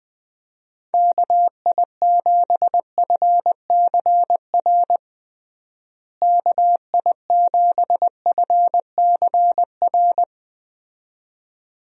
Morse code, heard as a keyed tone, about 21 wpm, 700 Hz. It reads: KI7FCR KI7FCR
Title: USA Call Signs - 6 Characters - Repeated 2x - 20wpm